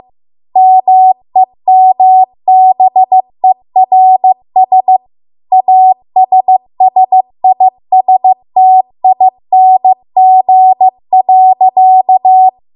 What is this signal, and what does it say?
Morse code, heard as a keyed tone, about 15 wpm, 750 Hz. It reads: MEMBERS ASSISTING.